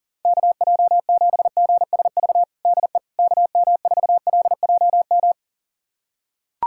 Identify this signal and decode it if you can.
Morse code, heard as a keyed tone, about 40 wpm, 700 Hz. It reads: KJ7GSV DE KM4LJM
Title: USA Callsign DE Callsign - 6 Characters - 40wpm